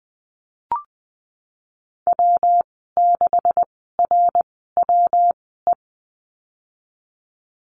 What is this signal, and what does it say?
Morse code, heard as a keyed tone, about 20 wpm, spaced as for 15 wpm, 700 Hz. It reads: W6RWE
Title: 147 Lesson 24g - Letter y - Call Signs 20wpm